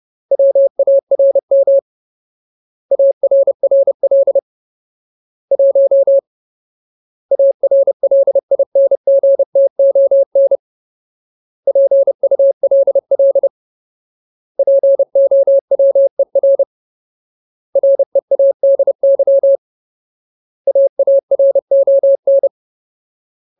Morse code, heard as a keyed tone, about 30 wpm, 550 Hz. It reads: WARM ARRL 1 ARLINGTON PULL POWER READY AARON
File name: Ultimate Word List / 30WPM Ultimate